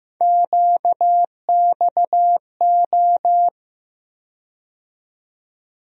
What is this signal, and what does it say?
Morse code, heard as a keyed tone, about 15 wpm, 700 Hz. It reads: QXO
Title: USA Call Sign Postfixes - 3 Letters 15wpm